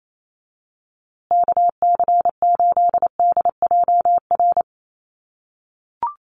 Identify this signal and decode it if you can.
Morse code, heard as a keyed tone, about 28 wpm, 700 Hz. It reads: KC8DJR
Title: USA Call Signs Repeated 2x 28wpm